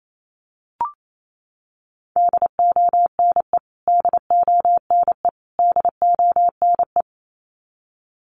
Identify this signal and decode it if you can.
Morse code, heard as a keyed tone, about 28 wpm, 700 Hz. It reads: DONE DONE DONE
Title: Top 500 Words Repeated 28wpm